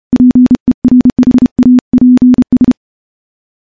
Morse code, heard as a keyed tone, about 28 wpm, 250 Hz. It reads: PERHAPS
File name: morse / morse-training-30WPM